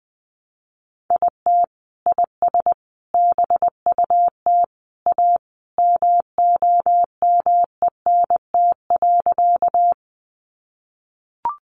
Morse code, heard as a keyed tone, about 20 wpm, 700 Hz. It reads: IT IS BUT A MOMENT.
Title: Sentences from Top 1000 Words 20wpm